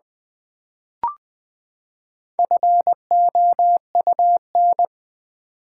Morse code, heard as a keyed tone, about 20 wpm, 700 Hz. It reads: FOUN
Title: Build A Word - 4 Letters At A Time - 10-Letter Words - 20wpm